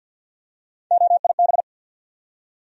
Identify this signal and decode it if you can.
Morse code, heard as a keyed tone, about 52 wpm, 700 Hz. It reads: OIB
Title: USA Call Sign Postfixes - 3 Letters 50wpm